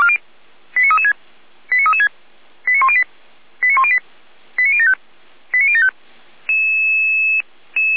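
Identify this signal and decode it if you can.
Five-tone selective call (ZVEI), heard as quick runs of five tones, tones of about 70 ms. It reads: 89336 89336 89108 89108 89075 89075
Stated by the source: ZVEI five-tone selective call received by SDR.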